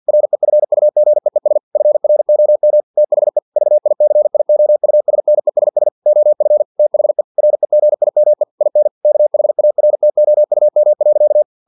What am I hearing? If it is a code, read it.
Morse code, heard as a keyed tone, about 50 wpm, 600 Hz. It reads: REFUGEES FROM THE VICIOUSNESS OF THE REGIME IN KHARTOUM.
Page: Never 49wpm